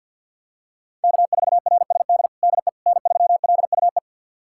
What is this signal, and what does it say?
Morse code, heard as a keyed tone, about 51 wpm, 700 Hz. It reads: K4RSD DE N3LUE
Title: USA Callsign DE Callsign - 5 Characters - 50wpm